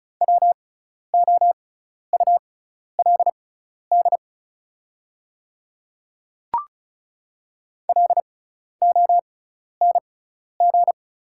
Morse code, heard as a keyed tone, about 36 wpm, spaced as for 12 wpm, 700 Hz. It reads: WOULD LONG
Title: 6x Character Spacing - Top 100 Words - 35wpm